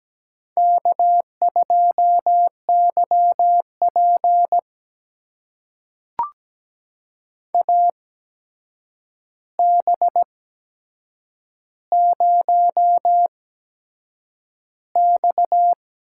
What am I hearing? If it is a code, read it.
Morse code, heard as a keyed tone, about 17 wpm, spaced as for 7 wpm, 700 Hz. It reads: K2YP AB0X
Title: USA Call Signs - 8x Character Spacing - 17wpm